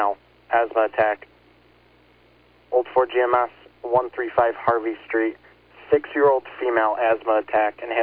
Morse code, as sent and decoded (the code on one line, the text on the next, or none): none